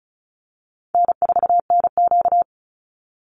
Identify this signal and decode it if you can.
Morse code, heard as a keyed tone, about 34 wpm, 700 Hz. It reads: N4NQ
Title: Call Signs Repeated 3x - 35wpm